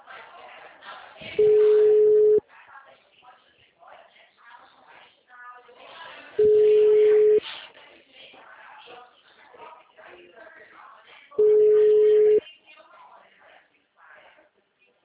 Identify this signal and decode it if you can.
Ringback tone (Europe, 425 Hz), with speech.